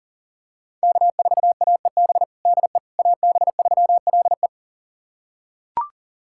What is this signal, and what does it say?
Morse code, heard as a keyed tone, about 37 wpm, 700 Hz. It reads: K4AEB DE AB3LE
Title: USA Callsign DE Callsign - 5 Characters - 40wpm